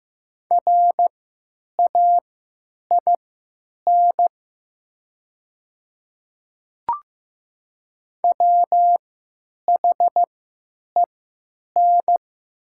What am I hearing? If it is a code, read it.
Morse code, heard as a keyed tone, about 15 wpm, spaced as for 9 wpm, 700 Hz. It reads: RAIN WHEN
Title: 3x Character Spacing - Top 300 Words - 15wpm